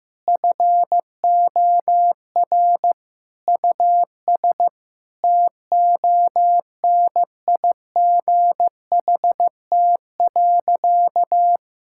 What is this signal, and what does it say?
Morse code, heard as a keyed tone, about 15 wpm, 700 Hz. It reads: FOR US TONIGHT.
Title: Sentences from Top 1000 Words - Review Part 5 - 15wpm